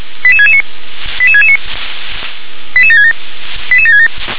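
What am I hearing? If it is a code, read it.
Five-tone selective call (ZVEI), heard as quick runs of five tones, tones of about 70 ms. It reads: 88669 88669 88767 88767
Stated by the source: ZVEI five-tone selective call 88725 received on SDR.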